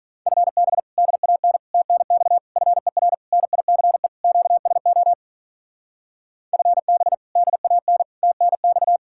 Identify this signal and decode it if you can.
Morse code, heard as a keyed tone, about 47 wpm, 700 Hz. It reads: FB DAN TNX FER NICE QSO FB DAN TNX